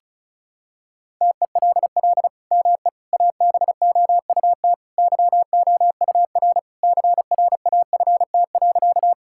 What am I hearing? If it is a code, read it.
Morse code, heard as a keyed tone, about 36 wpm, 700 Hz. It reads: TELL ME ABOUT YOUR CRAFT.
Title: Sentences from Top 1000 Words - Review Part 6 - 35wpm